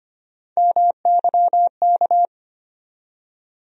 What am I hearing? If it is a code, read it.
Morse code, heard as a keyed tone, about 25 wpm, 700 Hz. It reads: MYK
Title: USA Call Sign Postfixes - 3 Letters 25wpm